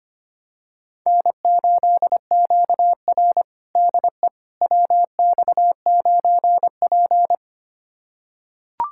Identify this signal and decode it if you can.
Morse code, heard as a keyed tone, about 25 wpm, 700 Hz. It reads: N8QR DE WX9P
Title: USA Callsign DE Callsign - 4 Characters - 25wpm